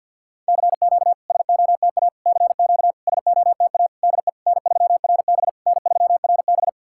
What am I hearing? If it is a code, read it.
Morse code, heard as a keyed tone, about 51 wpm, 690 Hz. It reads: CQ SOTA CQ SOTA DE N3RB N3RB
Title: SOTA QSO Practice 50wpm